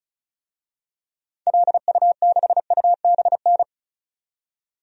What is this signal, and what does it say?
Morse code, heard as a keyed tone, about 34 wpm, 700 Hz. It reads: LU6UBN
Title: DXCC-100 Call Signs - 6 Characters - 35wpm